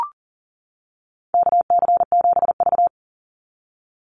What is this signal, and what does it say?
Morse code, heard as a keyed tone, about 40 wpm, 700 Hz. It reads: KC7V
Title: Call Signs - 40wpm